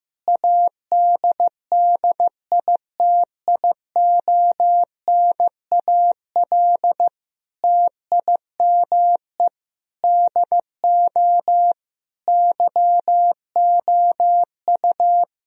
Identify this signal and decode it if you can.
Morse code, heard as a keyed tone, about 15 wpm, 700 Hz. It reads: ADDITIONAL TIME DO YOU
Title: Sentences from Top 1000 Words - Review Part 4 - 15wpm